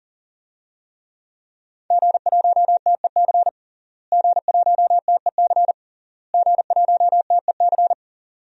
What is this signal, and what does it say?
Morse code, heard as a keyed tone, about 40 wpm, 700 Hz. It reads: G1TEC G1TEC G1TEC
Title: DXCC-100 Call Signs - 5 Characters - Repeated 3x - 40wpm